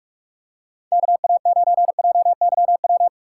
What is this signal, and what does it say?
Morse code, heard as a keyed tone, about 46 wpm, 700 Hz. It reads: KA9JYW
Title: USA Call Signs - 6 Characters - 2x Character Spacing - 45wpm